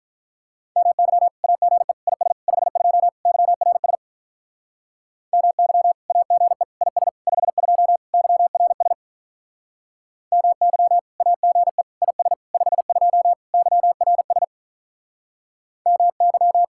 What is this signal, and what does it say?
Morse code, heard as a keyed tone, about 45 wpm, 700 Hz. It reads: MY AGE IS 52 YRS MY AGE IS 52 YRS MY AGE IS 52 YRS MY